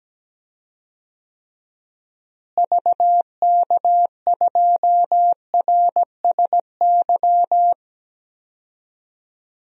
Morse code, heard as a keyed tone, about 17 wpm, 700 Hz. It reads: VK2RSY
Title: DXCC-100 Call Signs - 6 Characters - Repeated 2x - 17wpm